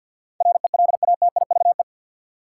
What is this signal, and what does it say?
Morse code, heard as a keyed tone, about 53 wpm, 700 Hz. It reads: RELATIVE
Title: Sentences from Top 1400 Words - New Word Drill Alt - 50wpm